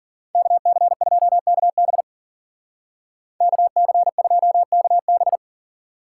Morse code, heard as a keyed tone, about 53 wpm, 700 Hz. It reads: KC2KB KC2KB